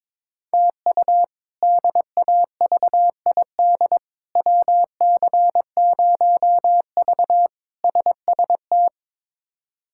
Morse code, heard as a keyed tone, about 22 wpm, 700 Hz. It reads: TU DAVID WC0V SST
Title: K1USN SST QSO Practice - 22wpm